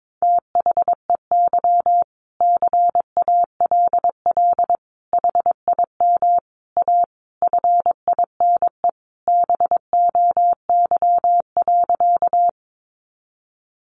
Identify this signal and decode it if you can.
Morse code, heard as a keyed tone, about 22 wpm, 700 Hz. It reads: THEY CALL HIM A FINE BOY.
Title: Sentences from Top 500 Words 22wpm